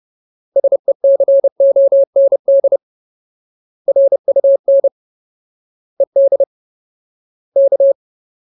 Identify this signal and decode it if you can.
Morse code, heard as a keyed tone, about 30 wpm, 550 Hz. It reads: SECOND RUN ED K
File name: Ultimate Word List / 30WPM Ultimate